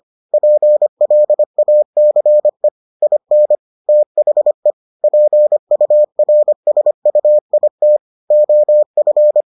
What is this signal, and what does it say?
Morse code, heard as a keyed tone, about 25 wpm, 600 Hz. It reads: PLACE IN THE PURSUIT OF